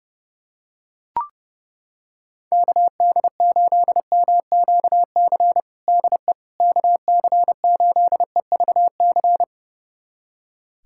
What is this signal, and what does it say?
Morse code, heard as a keyed tone, about 30 wpm, 700 Hz. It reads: KD8MQC DE KC8EVC
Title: USA Callsign DE Callsign 30wpm